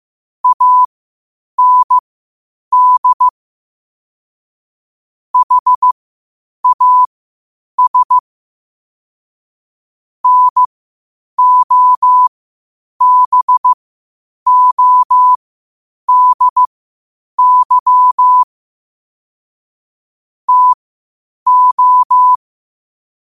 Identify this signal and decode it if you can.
Morse code, heard as a keyed tone, about 15 wpm, spaced as for 8 wpm, 1000 Hz. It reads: AND HAS NOBODY TO
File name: qotdfiles / 8wpm